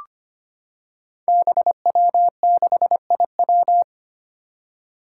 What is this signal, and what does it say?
Morse code, heard as a keyed tone, about 25 wpm, 700 Hz. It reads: BW6IW
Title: Five Challenging Letter-Number Combinations 25wpm